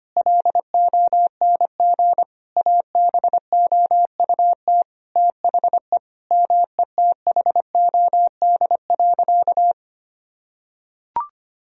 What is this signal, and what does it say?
Morse code, heard as a keyed tone, about 25 wpm, 700 Hz. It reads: LONG ABOUT THE METHOD.
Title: Sentences from Top 1000 Words - Review Part 3 - 25wpm